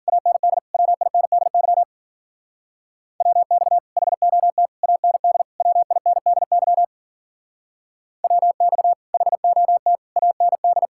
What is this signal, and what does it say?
Morse code, heard as a keyed tone, about 48 wpm, 700 Hz. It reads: AND WINDY WX HOT AND WINDY WX HOT AND
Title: Speed-Racing Common QSO Phrases - 40wpm